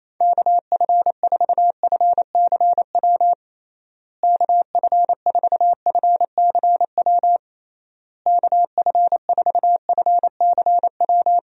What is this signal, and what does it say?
Morse code, heard as a keyed tone, about 28 wpm, 700 Hz. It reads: KF4FCW KF4FCW KF4FCW
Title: USA Call Signs - 6 Characters - Repeated 3x - 28wpm